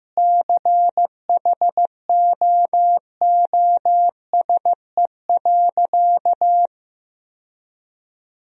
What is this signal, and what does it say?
Morse code, heard as a keyed tone, about 15 wpm, 700 Hz. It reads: CHOOSE.
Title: Sentences from Top 1000 Words - Review Part 4 - 15wpm